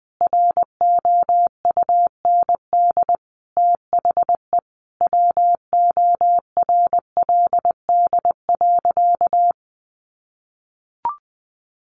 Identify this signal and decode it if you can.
Morse code, heard as a keyed tone, about 20 wpm, 700 Hz. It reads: ROUND THE WORLD.
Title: Sentences from Top 600 Words 20wpm